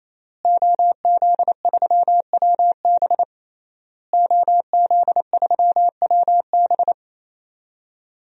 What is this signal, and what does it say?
Morse code, heard as a keyed tone, about 28 wpm, 700 Hz. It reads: OZ3WB OZ3WB
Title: DXCC-100 Call Signs - 5 Characters - Repeated 3x - 28wpm